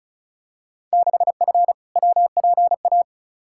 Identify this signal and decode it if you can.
Morse code, heard as a keyed tone, about 36 wpm, 700 Hz. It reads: 6F WPA
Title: Field Day Exchanges Encore - 35wpm